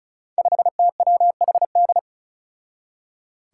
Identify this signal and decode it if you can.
Morse code, heard as a keyed tone, about 36 wpm, 700 Hz. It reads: 5TWHD
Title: Five Challenging Letter-Number Combinations 35wpm